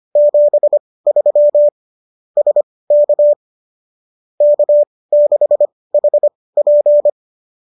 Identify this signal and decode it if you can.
Morse code, heard as a keyed tone, about 25 wpm, spaced as for 18 wpm, 590 Hz. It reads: 73 SK K6HP